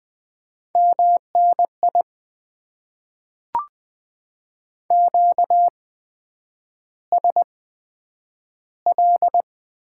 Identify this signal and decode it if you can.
Morse code, heard as a keyed tone, about 20 wpm, spaced as for 9 wpm, 700 Hz. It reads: MNI QSL
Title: Top 90 QSO Elements - 8x Character Spacing - 20wpm